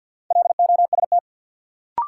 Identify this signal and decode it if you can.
Morse code, heard as a keyed tone, about 47 wpm, 700 Hz. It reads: LOST
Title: Sentences from Top 1400 Words - New Word Drill - 50wpm